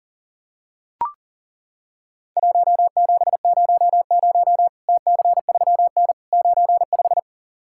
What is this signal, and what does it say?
Morse code, heard as a keyed tone, about 37 wpm, 700 Hz. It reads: 1700 TC3N 95
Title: Worked All Europe DX Contest - QTC Exchange - 40wpm